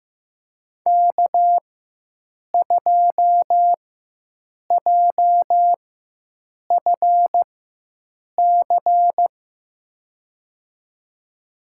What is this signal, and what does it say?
Morse code, heard as a keyed tone, about 16 wpm, spaced as for 7 wpm, 700 Hz. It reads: K2JFC
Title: USA Call Signs - 4x Character Spacing - 15wpm